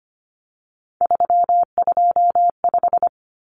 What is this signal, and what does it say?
Morse code, heard as a keyed tone, about 25 wpm, 700 Hz. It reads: 325
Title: Numbers - 3-Digit 25wpm